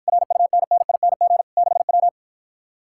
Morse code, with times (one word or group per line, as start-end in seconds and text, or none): 0.1-1.4 RUNNING
1.6-2.1 6W